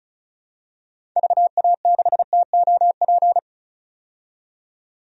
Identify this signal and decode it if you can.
Morse code, heard as a keyed tone, about 36 wpm, 700 Hz. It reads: VA6TOP
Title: DXCC-100 Call Signs - 6 Characters - 35wpm